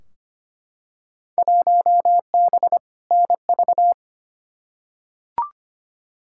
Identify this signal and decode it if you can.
Morse code, heard as a keyed tone, about 25 wpm, 700 Hz. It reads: 1B NV
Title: Field Day Exchanges - 25wpm